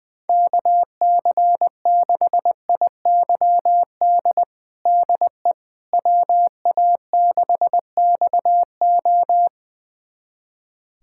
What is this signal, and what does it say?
Morse code, heard as a keyed tone, about 20 wpm, 700 Hz. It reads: KC6IYD DE WA6XO
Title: USA Callsign DE Callsign 20wpm